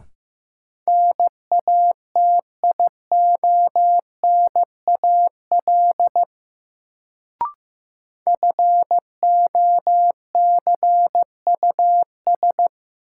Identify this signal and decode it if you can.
Morse code, heard as a keyed tone, about 15 wpm, 700 Hz. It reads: NATIONAL FOCUS
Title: Sentences from Top 1000 Words - Review Part 4 - New Word Drill - 15wpm